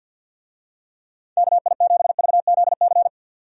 Morse code, heard as a keyed tone, about 52 wpm, 700 Hz. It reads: KI7VZC
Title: USA Call Signs - 6 Characters - 50wpm